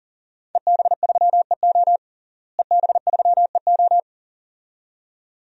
Morse code, heard as a keyed tone, about 40 wpm, 700 Hz. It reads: EB3EO EB3EO
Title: DXCC-100 Call Signs - 5 Characters - Repeated 2x - 40wpm